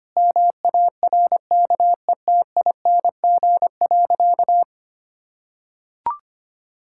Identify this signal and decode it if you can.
Morse code, heard as a keyed tone, about 25 wpm, 700 Hz. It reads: MARKETING.
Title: Sentences from Top 1000 Words - Review Part 6 - 25wpm